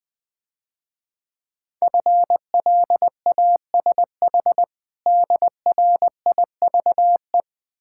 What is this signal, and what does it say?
Morse code, heard as a keyed tone, about 20 wpm, 700 Hz. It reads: FLASH DRIVE
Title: Difficult Words - Speed-Racing with Hints - 20wpm